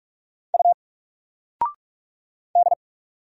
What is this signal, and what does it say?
Morse code, heard as a keyed tone, about 44 wpm, 700 Hz. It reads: U D
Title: Single Letter-Number Alt 45wpm